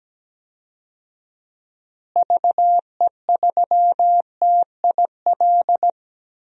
Morse code, heard as a keyed tone, about 17 wpm, 700 Hz. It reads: VE3TIL
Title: DXCC-100 Call Signs - 6 Characters - Repeated 3x - 17wpm